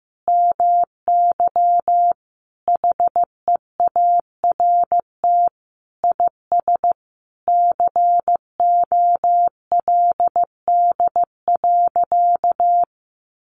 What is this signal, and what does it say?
Morse code, heard as a keyed tone, about 15 wpm, 700 Hz. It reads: MY HEART IS COLD.
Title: Sentences from Top 1200 Words 15wpm